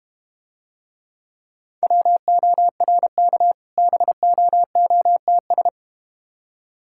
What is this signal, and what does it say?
Morse code, heard as a keyed tone, about 32 wpm, 700 Hz. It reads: WORK BOOTS